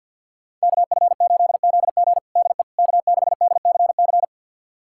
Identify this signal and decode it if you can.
Morse code, heard as a keyed tone, about 53 wpm, 700 Hz. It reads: KF8ZG DE K6DCC
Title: USA Callsign DE Callsign - 5 Characters - 50wpm